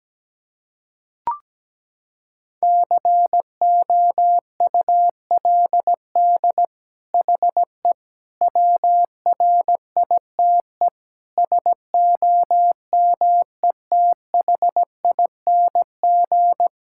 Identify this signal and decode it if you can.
Morse code, heard as a keyed tone, about 17 wpm, 700 Hz. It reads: COULD HE WRITE SOMETHING
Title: Sentences from Top 1000 Words - Review Part 6 - 17wpm